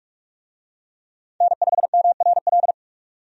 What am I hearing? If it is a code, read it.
Morse code, heard as a keyed tone, about 46 wpm, 700 Hz. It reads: N5MRL
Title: USA Call Signs - 5 Characters - 45wpm